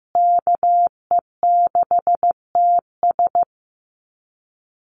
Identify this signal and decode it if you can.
Morse code, heard as a keyed tone, about 15 wpm, 700 Hz. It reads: KE6TS
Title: USA Call Signs Repeated 3x 15wpm